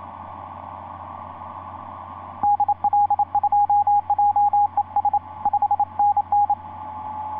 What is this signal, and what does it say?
Morse code, heard as a keyed tone, about 27 wpm, 820 Hz. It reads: DL2JES 5NN
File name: 9x0lx sp1mwk etc 12m cw 12feb2010 1146